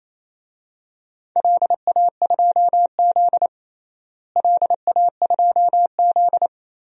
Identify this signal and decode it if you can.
Morse code, heard as a keyed tone, about 28 wpm, 700 Hz. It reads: LA2Z LA2Z
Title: DXCC-100 Call Signs - 4 Characters - Repeated 3x - 28wpm